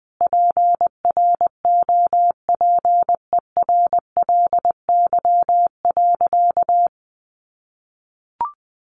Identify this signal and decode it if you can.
Morse code, heard as a keyed tone, about 20 wpm, 700 Hz. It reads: PROPERLY.
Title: Sentences from Top 1100 Words 20wpm